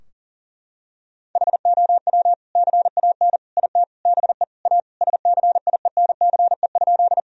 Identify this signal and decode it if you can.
Morse code, heard as a keyed tone, about 40 wpm, 700 Hz. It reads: HOW CAN IT BE A SCIENCE?
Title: Sentences from Top 1000 Words - Review Part 2 - 40wpm